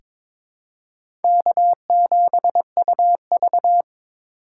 Morse code, heard as a keyed tone, about 22 wpm, 700 Hz. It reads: K7UV
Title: USA Call Signs - 2x Character Spacing - 22wpm